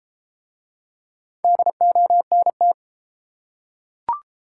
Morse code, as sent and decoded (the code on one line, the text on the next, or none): -.. --- -. -
DONT